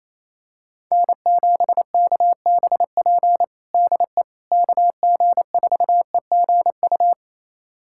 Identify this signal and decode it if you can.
Morse code, heard as a keyed tone, about 27 wpm, 700 Hz. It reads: N7KBP DE KG4EGU
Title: USA Callsign DE Callsign 28wpm